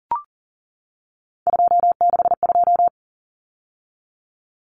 Morse code, heard as a keyed tone, about 40 wpm, 700 Hz. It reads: 262
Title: Numbers - 3-Digit 40wpm